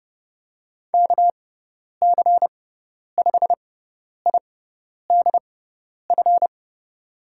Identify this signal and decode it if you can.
Morse code, heard as a keyed tone, about 30 wpm, spaced as for 10 wpm, 700 Hz. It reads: KC5IDF
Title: USA Call Signs - 6 Characters - 6x Character Spacing - 30wpm